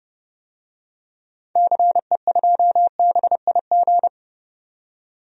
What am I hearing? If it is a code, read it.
Morse code, heard as a keyed tone, about 30 wpm, 700 Hz. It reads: CE2BIG
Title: DXCC-100 Call Signs - 6 Characters - 30wpm